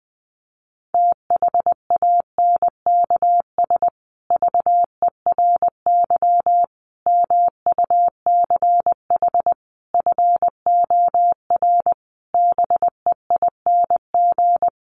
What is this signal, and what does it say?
Morse code, heard as a keyed tone, about 20 wpm, 700 Hz. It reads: THANKS VERY MUCH FOR BEING
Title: Sets of 5 Words Encore 20wpm